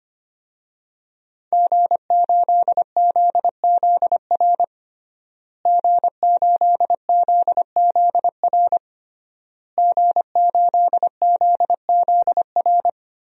Morse code, heard as a keyed tone, about 25 wpm, 700 Hz. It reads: G8ZZR G8ZZR G8ZZR
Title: DXCC-100 Call Signs - 5 Characters - Repeated 3x - 25wpm